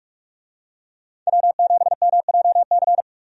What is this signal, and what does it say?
Morse code, heard as a keyed tone, about 46 wpm, 700 Hz. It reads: W7MJC
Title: USA Call Signs - 5 Characters - 45wpm